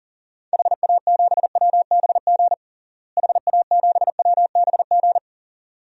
Morse code, heard as a keyed tone, about 40 wpm, 700 Hz. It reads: HA7WBG HA7WBG
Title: DXCC-100 Call Signs - 6 Characters - Repeated 2x - 40wpm